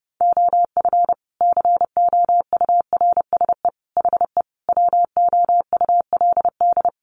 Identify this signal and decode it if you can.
Morse code, heard as a keyed tone, about 30 wpm, 700 Hz. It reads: OF COURSE HE WOULD
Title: Sentences from Top 1200 Words 30wpm